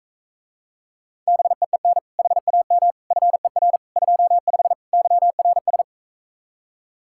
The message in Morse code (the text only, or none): BEEN HAM FER 25 YRS